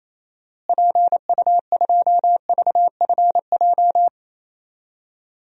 Morse code, heard as a keyed tone, about 28 wpm, 700 Hz. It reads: PU2VFJ
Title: DXCC-100 Call Signs - 6 Characters - Repeated 2x - 28wpm